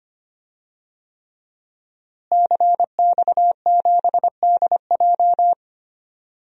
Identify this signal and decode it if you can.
Morse code, heard as a keyed tone, about 25 wpm, 700 Hz. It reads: CX7DJ
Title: DXCC-100 Call Signs - 5 Characters - Repeated 3x - 25wpm